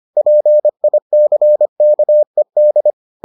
Morse code, heard as a keyed tone, about 25 wpm, 600 Hz. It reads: PICKED